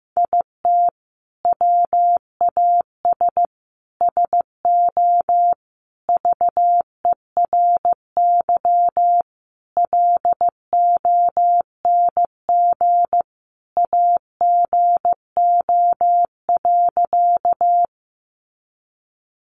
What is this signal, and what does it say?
Morse code, heard as a keyed tone, about 15 wpm, 700 Hz. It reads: IT WAS SO VERY LONG AGO.
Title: Sentences from Top 500 Words 15wpm